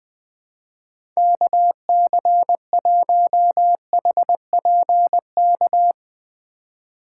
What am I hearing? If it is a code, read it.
Morse code, heard as a keyed tone, about 20 wpm, 700 Hz. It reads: KC1HPK
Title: USA Call Signs - 6 Characters - 4x Character Spacing - 20wpm